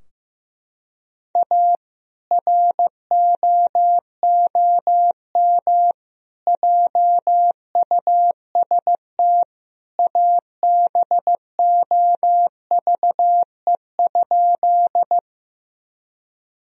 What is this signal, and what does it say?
Morse code, heard as a keyed tone, about 15 wpm, 700 Hz. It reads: A ROOM JUST ABOVE?
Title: Sentences from Top 1000 Words - Review Part 2 - 15wpm